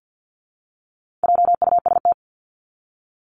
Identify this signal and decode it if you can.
Morse code, heard as a keyed tone, about 50 wpm, 700 Hz. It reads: JUST
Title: Top 300 Words Repeated 50wpm